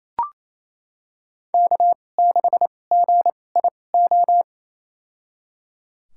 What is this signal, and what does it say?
Morse code, heard as a keyed tone, about 28 wpm, spaced as for 20 wpm, 700 Hz. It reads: K6GIO
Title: USA Call Signs - 5 Characters - 2x Character Spacing - 28wpm